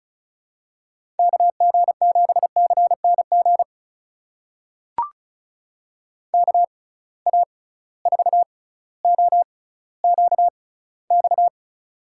Morse code, heard as a keyed tone, about 35 wpm, spaced as for 18 wpm, 700 Hz. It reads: KG7CNG KA4OQX
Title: USA Call Signs - 6 Characters - 6x Character Spacing - 35wpm